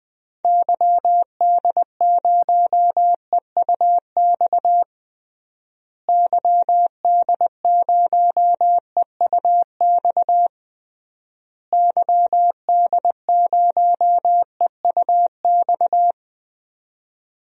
Morse code, heard as a keyed tone, about 20 wpm, 700 Hz. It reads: YD0EUX YD0EUX YD0EUX
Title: DXCC-100 Call Signs - 6 Characters - Repeated 3x - 20wpm